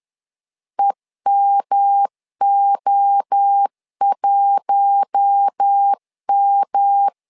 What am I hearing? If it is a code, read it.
Morse code, heard as a keyed tone, about 10 wpm, 800 Hz. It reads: EMO1M
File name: LEZIONI / 4 Q1 10wpm 2448